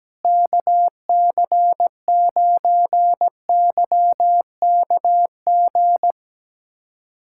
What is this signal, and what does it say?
Morse code, heard as a keyed tone, about 17 wpm, 700 Hz. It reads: KC9YKG
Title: USA Callsign DE Callsign - 6 Characters - 17wpm